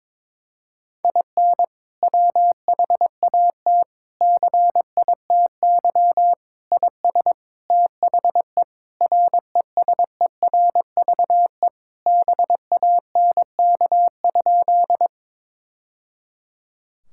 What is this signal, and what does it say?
Morse code, heard as a keyed tone, about 22 wpm, 700 Hz. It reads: IN WHAT CITY IS THE RESERVE BANK?
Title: Sentences from Top 1000 Words - Review Part 6 - 22wpm